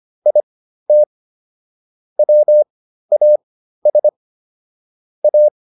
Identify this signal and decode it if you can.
Morse code, heard as a keyed tone, about 25 wpm, spaced as for 13 wpm, 600 Hz. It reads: IT WAS A